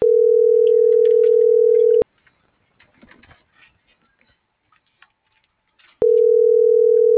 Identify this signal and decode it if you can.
Ringback tone (North America), 440 + 480 Hz.